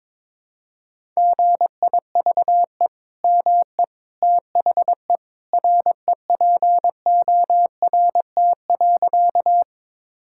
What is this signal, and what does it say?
Morse code, heard as a keyed tone, about 22 wpm, 700 Hz. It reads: GIVE ME THE REPORT.
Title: Sentences from Top 1000 Words - Review Part 6 - 22wpm